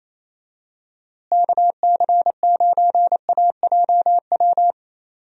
Morse code, heard as a keyed tone, about 28 wpm, 700 Hz. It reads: KC9AJW
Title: USA Call Signs - 6 Characters - Repeated 3x - 28wpm